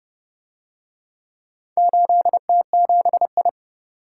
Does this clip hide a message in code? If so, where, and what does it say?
Yes: Morse code from 1.8 to 3.5 s, heard as a keyed tone, about 29 wpm, 700 Hz. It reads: 8T7I